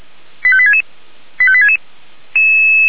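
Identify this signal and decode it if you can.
Five-tone selective call (ZVEI), heard as quick runs of five tones, tones of about 70 ms. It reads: 85866 85866
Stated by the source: ZVEI five-tone call 85864 received by SDR.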